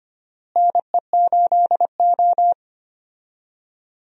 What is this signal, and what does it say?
Morse code, heard as a keyed tone, about 25 wpm, 700 Hz. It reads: NE8O
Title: USA Call Signs - 4 Characters - Repeated 2x - 25wpm